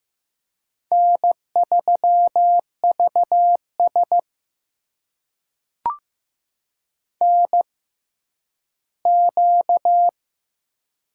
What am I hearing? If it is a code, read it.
Morse code, heard as a keyed tone, about 15 wpm, spaced as for 10 wpm, 700 Hz. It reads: N3VS NQ
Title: USA Call Signs - 4 Characters - 6x Character Spacing - 15wpm